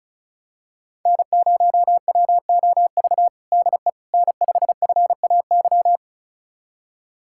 Morse code, heard as a keyed tone, about 36 wpm, 700 Hz. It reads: N0WOV DE N5FAY
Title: USA Callsign DE Callsign - 5 Characters - 35wpm